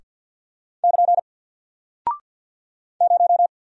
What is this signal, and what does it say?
Morse code, heard as a keyed tone, about 48 wpm, 700 Hz. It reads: <KN> 0
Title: Single Character (Letter:Number:Punctuation:Prosign) Alt 50wpm